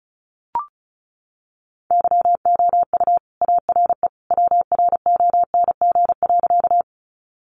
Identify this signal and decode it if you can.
Morse code, heard as a keyed tone, about 34 wpm, 700 Hz. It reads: YOU ARE WRONG.
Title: Sentences from Top 900 Words 35wpm